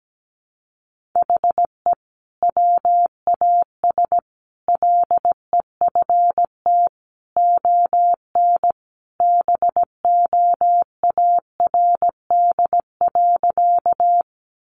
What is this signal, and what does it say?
Morse code, heard as a keyed tone, about 17 wpm, 700 Hz. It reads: HE WAS LEFT ON BOARD.
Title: Sentences from Top 700 Words 17wpm